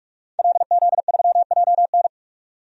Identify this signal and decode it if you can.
Morse code, heard as a keyed tone, about 41 wpm, 700 Hz. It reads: LZ3JN
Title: DXCC-100 Call Signs - 5 Characters - 45wpm